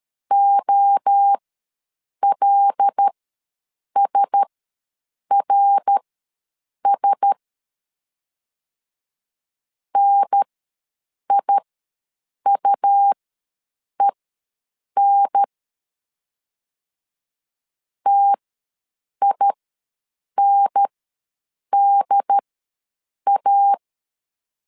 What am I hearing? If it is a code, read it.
Morse code, heard as a keyed tone, about 13 wpm, spaced as for 7 wpm, 800 Hz. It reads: OLSRS NIUEN TINDA
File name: LEZIONI / 5 LBVC / 5 L ws 12wpm 2958